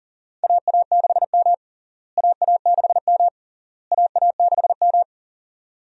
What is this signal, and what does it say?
Morse code, heard as a keyed tone, about 40 wpm, 700 Hz. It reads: AA6M AA6M AA6M
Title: USA Call Signs - 4 Characters - Repeated 3x - 40wpm